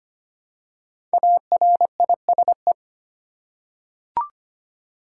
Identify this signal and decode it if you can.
Morse code, heard as a keyed tone, about 25 wpm, 700 Hz. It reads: ARISE
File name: 5 Letter Words - First 3 Letters Given - 25wpm